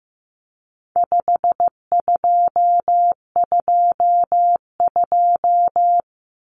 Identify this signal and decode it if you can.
Morse code, heard as a keyed tone, about 15 wpm, 700 Hz. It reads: 5222
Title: Numbers - 4-Digit 15wpm